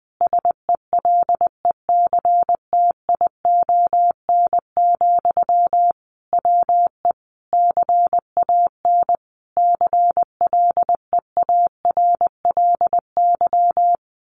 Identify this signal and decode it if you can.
Morse code, heard as a keyed tone, about 20 wpm, 700 Hz. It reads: SELECTION, WE CAN CLEARLY
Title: Sentences from Top 1400 Words 20wpm